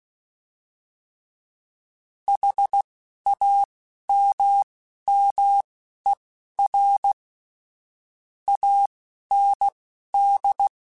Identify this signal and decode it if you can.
Morse code, heard as a keyed tone, about 16 wpm, spaced as for 11 wpm, 800 Hz. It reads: HAMMER AND
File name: Misc / Misc-mp3s / Word Pairs @10